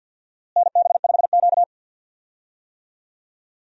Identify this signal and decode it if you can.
Morse code, heard as a keyed tone, about 47 wpm, 700 Hz. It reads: NB5Q
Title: USA Call Signs - 4 Characters - Repeated 3x - 50wpm